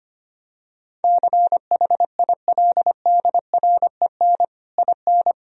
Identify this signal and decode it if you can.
Morse code, heard as a keyed tone, about 25 wpm, 700 Hz. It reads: CHILDREN IN
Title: Sentences from Top 1000 Words - Review Part 4 - 25wpm